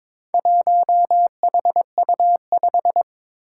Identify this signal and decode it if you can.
Morse code, heard as a keyed tone, about 22 wpm, 700 Hz. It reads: 1HU5
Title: Four Challenging Letter-Number Combinations 22wpm